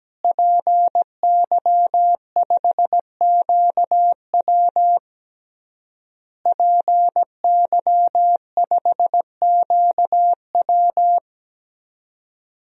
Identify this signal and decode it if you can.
Morse code, heard as a keyed tone, about 17 wpm, 700 Hz. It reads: PY5QW PY5QW
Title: DXCC-100 Call Signs - 5 Characters - Repeated 3x - 17wpm